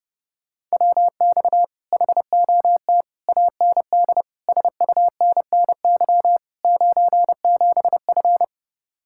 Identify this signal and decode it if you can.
Morse code, heard as a keyed tone, about 28 wpm, 700 Hz. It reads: WX HOT AND SUNNY 97F
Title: Speed-Racing Common QSO Phrases - 30wpm